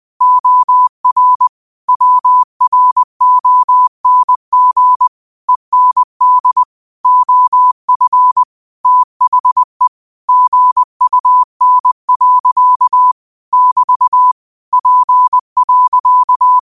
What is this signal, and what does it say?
Morse code, heard as a keyed tone, about 20 wpm, 1000 Hz. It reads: OR WRONG END OF THE GUN. <BT> P.